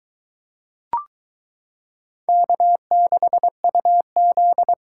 Morse code, heard as a keyed tone, about 22 wpm, 700 Hz. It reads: K6UZ